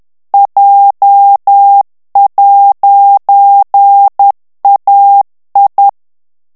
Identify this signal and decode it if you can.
Morse code, heard as a keyed tone, about 10 wpm, 800 Hz. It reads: J'AI
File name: morse mp3 / FRANCAIS TEXTE CLAIR 2 VITESSE 10 MOTS MINUTES